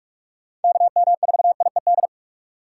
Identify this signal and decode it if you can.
Morse code, heard as a keyed tone, about 44 wpm, 700 Hz. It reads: KM4IED
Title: USA Call Signs - 6 Characters - 3x Character Spacing - 45wpm